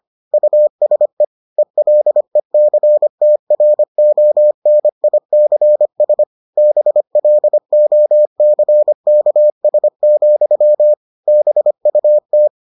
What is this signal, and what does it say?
Morse code, heard as a keyed tone, about 25 wpm, 600 Hz. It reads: USE ELECTRONICS BLOCKS, BUT